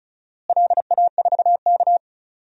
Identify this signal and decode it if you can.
Morse code, heard as a keyed tone, about 36 wpm, 700 Hz. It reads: LA4K
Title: DXCC-100 Call Signs - 4 Characters - Repeated 2x - 35wpm